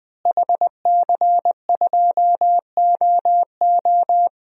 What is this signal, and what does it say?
Morse code, heard as a keyed tone, about 20 wpm, 700 Hz. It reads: HC2OO
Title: DXCC-100 Call Signs - 5 Characters - Repeated 2x - 20wpm